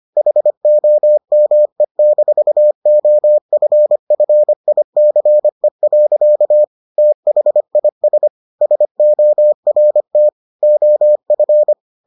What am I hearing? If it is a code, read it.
Morse code, heard as a keyed tone, about 25 wpm, 600 Hz. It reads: HOME-OFFICE. THIS SORT OF